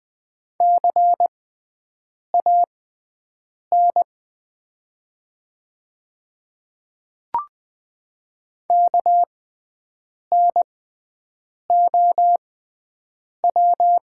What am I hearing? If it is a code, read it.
Morse code, heard as a keyed tone, about 20 wpm, spaced as for 7 wpm, 700 Hz. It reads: CAN KNOW
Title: 6x Character Spacing - Top 100 Words - 20wpm